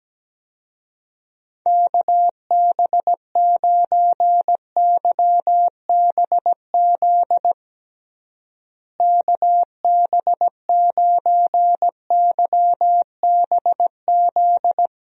Morse code, heard as a keyed tone, about 17 wpm, 700 Hz. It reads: KB9YBZ KB9YBZ
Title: USA Call Signs - 6 Characters - Repeated 3x - 17wpm